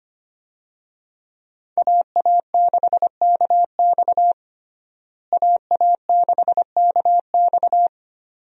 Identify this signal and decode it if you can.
Morse code, heard as a keyed tone, about 25 wpm, 700 Hz. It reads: AA6KX AA6KX
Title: USA Call Signs - 5 Characters - Repeated 2x - 25wpm